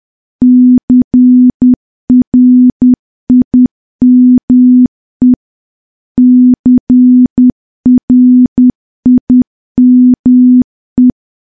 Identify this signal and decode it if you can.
Morse code, heard as a keyed tone, about 10 wpm, 250 Hz. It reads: CRIME CRIME
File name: morse / morse-training-10WPM